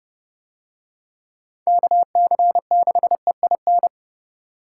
Morse code, heard as a keyed tone, about 29 wpm, 700 Hz. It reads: KC6EIN